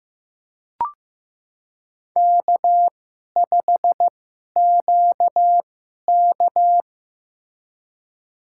Morse code, heard as a keyed tone, about 15 wpm, spaced as for 11 wpm, 700 Hz. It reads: K5QK
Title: USA Call Signs - 4 Characters - 2x Character Spacing - 15wpm